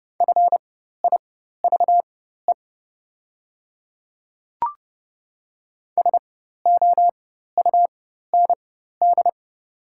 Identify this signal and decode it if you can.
Morse code, heard as a keyed tone, about 31 wpm, spaced as for 14 wpm, 700 Hz. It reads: FIVE SOUND
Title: 4x Character Spacing - Top 500 Words - 30wpm